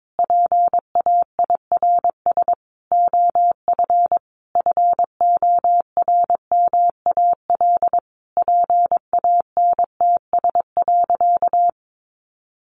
Morse code, heard as a keyed tone, about 22 wpm, 700 Hz. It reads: PAIRS OF FORMAL PANTS.
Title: Additional Easy Phrases 22wpm